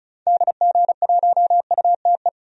Morse code, heard as a keyed tone, about 36 wpm, 700 Hz. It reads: DG1UTE
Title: DXCC-100 Call Signs - 6 Characters - 35wpm